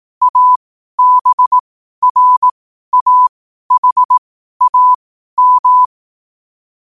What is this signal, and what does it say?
Morse code, heard as a keyed tone, about 18 wpm, spaced as for 12 wpm, 1000 Hz. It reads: ABRAHAM
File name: qotdfiles / 12wpm